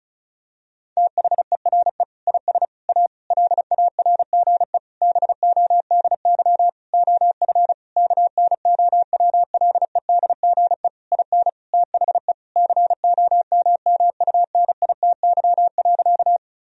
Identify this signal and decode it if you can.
Morse code, heard as a keyed tone, about 36 wpm, 700 Hz. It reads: THERE IS A LARGE BODY OF KNOWLEDGE IN THE COMMUNITY.
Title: Sentences from Top 1000 Words - Review Part 4 - 35wpm